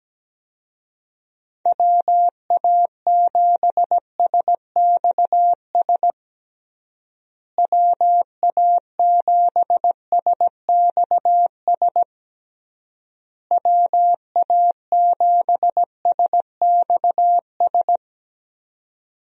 Morse code, heard as a keyed tone, about 17 wpm, 700 Hz. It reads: WA7SXS WA7SXS WA7SXS
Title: USA Call Signs - 6 Characters - Repeated 3x - 17wpm